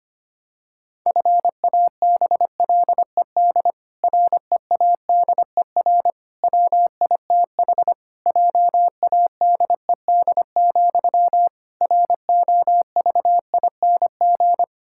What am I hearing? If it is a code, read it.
Morse code, heard as a keyed tone, about 25 wpm, 700 Hz. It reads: FABLED READER WITH JADED, ROVING
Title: Pangram Challenge v2 - 25wpm